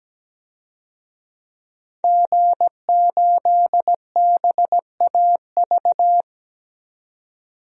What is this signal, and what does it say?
Morse code, heard as a keyed tone, about 17 wpm, 700 Hz. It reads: G8BAV
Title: DXCC-100 Call Signs - 5 Characters - 17wpm